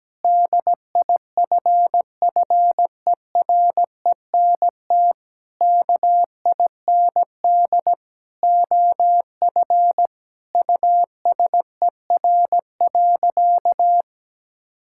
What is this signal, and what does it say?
Morse code, heard as a keyed tone, about 17 wpm, 700 Hz. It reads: DIFFERENT KIND OF USER.
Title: Sentences from Top 1000 Words - Review Part 5 - 17wpm